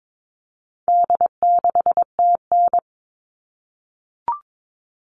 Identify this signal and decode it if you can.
Morse code, heard as a keyed tone, about 22 wpm, 700 Hz. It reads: D6TN
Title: All Cut Numbers - 4-Digit 22wpm